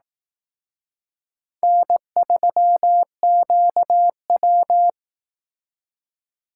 Morse code, heard as a keyed tone, about 18 wpm, 700 Hz. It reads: N3QW
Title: Speed Racing - USA 1x2 Call Signs 15wpm